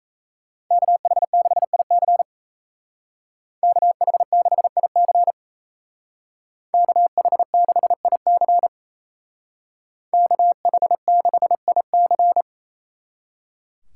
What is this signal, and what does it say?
Morse code, heard as a keyed tone, about 33 wpm, 700 Hz. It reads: KH6IC KH6IC KH6IC KH6IC
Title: Speed Racing - USA 2x2 Call Signs 28wpm